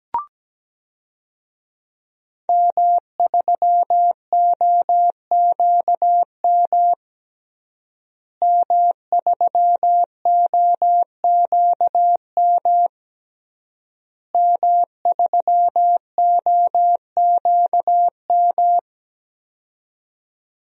Morse code, heard as a keyed tone, about 17 wpm, 700 Hz. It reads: M3OQM M3OQM M3OQM
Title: DXCC-100 Call Signs - 5 Characters - Repeated 3x - 17wpm